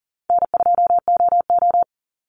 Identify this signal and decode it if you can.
Morse code, heard as a keyed tone, about 38 wpm, 700 Hz. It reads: N2OO
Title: Call Signs - 40wpm